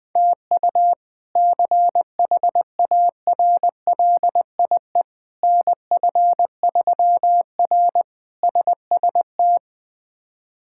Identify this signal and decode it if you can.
Morse code, heard as a keyed tone, about 20 wpm, 700 Hz. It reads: TU CHARLIE NF3R SST
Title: K1USN SST QSO Practice - 20wpm